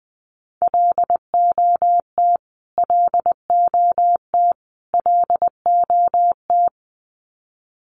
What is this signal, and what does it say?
Morse code, heard as a keyed tone, about 20 wpm, 700 Hz. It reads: LOT LOT LOT
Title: Top 500 Words Repeated 20wpm